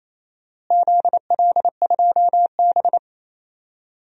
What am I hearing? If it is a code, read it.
Morse code, heard as a keyed tone, about 28 wpm, 700 Hz. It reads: ZL2B
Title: DXCC-100 Call Signs - 4 Characters - Repeated 2x - 28wpm